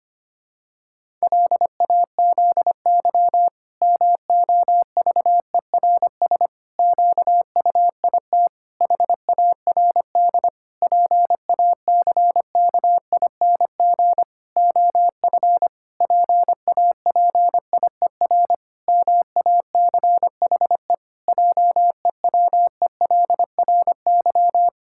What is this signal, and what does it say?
Morse code, heard as a keyed tone, about 25 wpm, 700 Hz. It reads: LAZY MOVERS QUIT HARD PACKING OF PAPIER MACHE JEWELRY
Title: Pangram Challenge v2 - 25wpm